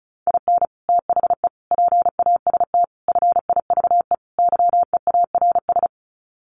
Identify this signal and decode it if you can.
Morse code, heard as a keyed tone, about 34 wpm, 700 Hz. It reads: IN THE PAST FIVE YEARS
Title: Sets of 5 Words Encore 35wpm